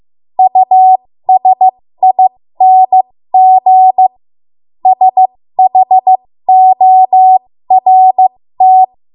Morse code, heard as a keyed tone, about 15 wpm, spaced as for 13 wpm, 750 Hz. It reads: USING SHORT
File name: Morse / 13WPM